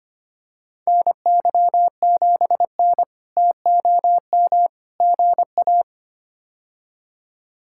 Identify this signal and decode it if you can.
Morse code, heard as a keyed tone, about 25 wpm, 700 Hz. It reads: NY7N TOM GA
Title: K1USN SST QSO Practice - 25wpm